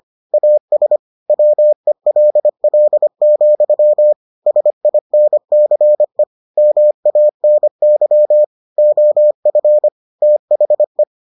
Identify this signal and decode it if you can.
Morse code, heard as a keyed tone, about 25 wpm, 600 Hz. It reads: AS WELL, SINCE MANY OF THE